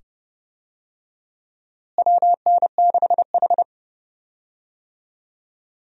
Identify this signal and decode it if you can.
Morse code, heard as a keyed tone, about 30 wpm, 700 Hz. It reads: WN6H
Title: USA Call Signs - 4 Characters - Repeated 2x - 30wpm